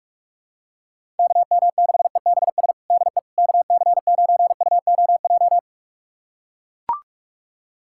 Morse code, heard as a keyed tone, about 44 wpm, 700 Hz. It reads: KM6EBS DE KC9UOJ
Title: USA Callsign DE Callsign - 6 Characters - 45wpm